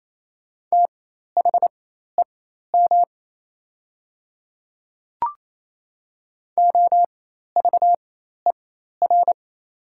Morse code, heard as a keyed tone, about 28 wpm, spaced as for 13 wpm, 700 Hz. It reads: THEM OVER
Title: 4x Character Spacing - Top 100 Words - 28wpm